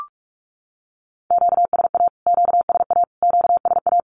Morse code, heard as a keyed tone, about 44 wpm, 700 Hz. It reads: QSA QSA QSA
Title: Q-Codes Repeated 3x 45wpm